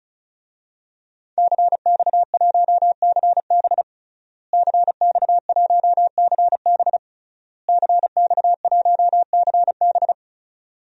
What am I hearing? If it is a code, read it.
Morse code, heard as a keyed tone, about 36 wpm, 700 Hz. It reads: CX1CB CX1CB CX1CB
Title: DXCC-100 Call Signs - 5 Characters - Repeated 3x - 35wpm